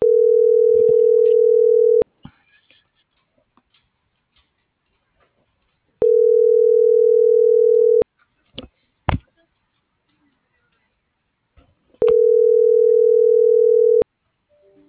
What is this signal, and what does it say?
Ringback tone (North America), 440 + 480 Hz.